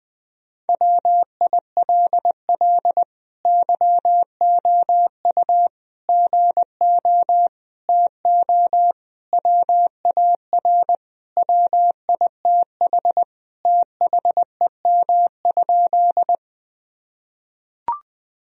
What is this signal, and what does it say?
Morse code, heard as a keyed tone, about 20 wpm, 700 Hz. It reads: WILL YOU GO TO WAR WITH THEM?
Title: Sentences from Top 1000 Words - Review Part 2 - 20wpm